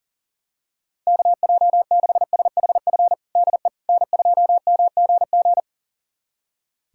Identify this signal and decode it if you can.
Morse code, heard as a keyed tone, about 40 wpm, 700 Hz. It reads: KJ6SHF DE N2MGG
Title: USA Callsign DE Callsign 40wpm